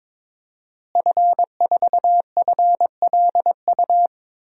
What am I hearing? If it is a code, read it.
Morse code, heard as a keyed tone, about 22 wpm, 700 Hz. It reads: F4FLU
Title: DXCC-100 Call Signs - 5 Characters - Repeated 3x - 22wpm